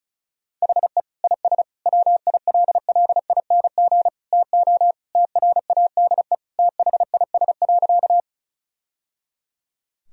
Morse code, heard as a keyed tone, about 36 wpm, 700 Hz. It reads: HE IS WILLING TO TRADE THIS.
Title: Sentences from Top 1000 Words - Review Part 4 - 35wpm